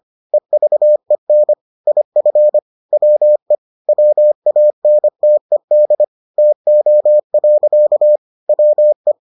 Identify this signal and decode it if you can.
Morse code, heard as a keyed tone, about 25 wpm, 600 Hz. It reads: EVEN IF WE WANTED TO. WE